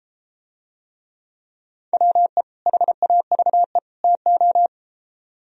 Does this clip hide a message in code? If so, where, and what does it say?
Yes: Morse code from 1.9 to 4.7 s, heard as a keyed tone, about 33 wpm, 700 Hz. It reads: WE HAVE TO